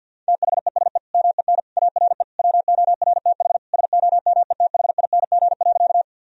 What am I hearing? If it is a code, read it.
Morse code, heard as a keyed tone, about 51 wpm, 700 Hz. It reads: THESE MEN ARE WORTH SOMETHING.
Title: Sentences from Top 1000 Words - Review Part 1 - 50wpm